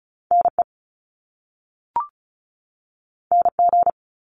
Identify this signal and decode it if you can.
Morse code, heard as a keyed tone, about 36 wpm, 700 Hz. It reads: NE NG
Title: Top Two Letter Combinations in Words 35wpm